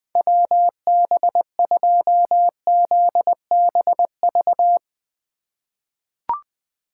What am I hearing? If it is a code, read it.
Morse code, heard as a keyed tone, about 20 wpm, 700 Hz. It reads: WB2ZBV
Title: USA Call Signs - 6 Characters - 6x Character Spacing - 20wpm